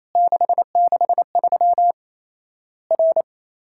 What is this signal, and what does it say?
Morse code, heard as a keyed tone, about 28 wpm, 700 Hz. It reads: 663 R
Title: Worked All Europe DX Contest - QTC Exchange - 28wpm